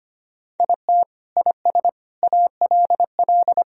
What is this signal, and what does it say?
Morse code, heard as a keyed tone, about 25 wpm, 700 Hz. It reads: IT IS ALL
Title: Sentences from Top 1000 Words - Review Part 4 - 25wpm